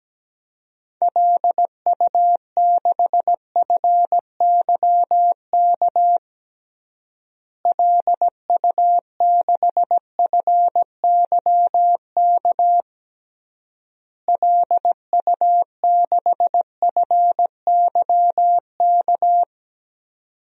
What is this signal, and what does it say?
Morse code, heard as a keyed tone, about 17 wpm, 700 Hz. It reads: LU6FYK LU6FYK LU6FYK
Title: DXCC-100 Call Signs - 6 Characters - Repeated 3x - 17wpm